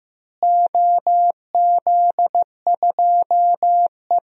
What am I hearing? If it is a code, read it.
Morse code, heard as a keyed tone, about 15 wpm, 700 Hz. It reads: OZ2E
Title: DXCC-100 Call Signs - 4 Characters - Repeated 2x - 15wpm